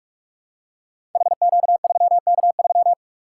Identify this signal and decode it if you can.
Morse code, heard as a keyed tone, about 44 wpm, 700 Hz. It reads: HQ3K3
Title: Code Groups - 5 Letter:Numbers - 45wpm